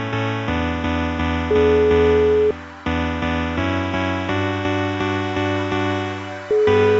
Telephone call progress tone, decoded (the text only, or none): ringback tone